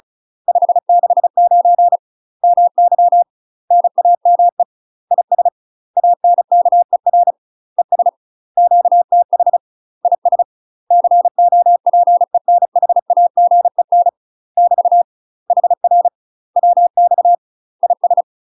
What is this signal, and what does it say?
Morse code, heard as a keyed tone, about 34 wpm, 700 Hz. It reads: 569 MY NAME IS ANKER ES QTH IS COPENHAGEN <BT> HR WX IS
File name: UK MASTER QSOs TRAINING / 35WPM / UK MASTER QSOs 0003